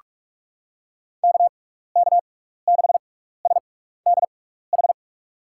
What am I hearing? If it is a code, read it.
Morse code, heard as a keyed tone, about 41 wpm, spaced as for 15 wpm, 700 Hz. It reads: KK6SDH
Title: USA Call Signs - 6x Character Spacing - 45wpm